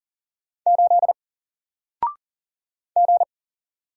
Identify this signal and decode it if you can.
Morse code, heard as a keyed tone, about 40 wpm, 700 Hz. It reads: 8 G
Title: Single Character (Letter:Number:Punctuation:Prosign) Alt 40wpm